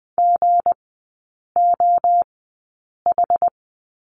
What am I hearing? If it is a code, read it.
Morse code, heard as a keyed tone, about 20 wpm, spaced as for 8 wpm, 690 Hz. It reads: GOH
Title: 159 Lesson 26e - Letter g - Three Characters 20wpm